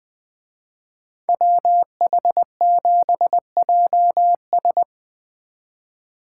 Morse code, heard as a keyed tone, about 23 wpm, 700 Hz. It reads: WH7JS
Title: Speed Racing - USA 2x2 Call Signs 20wpm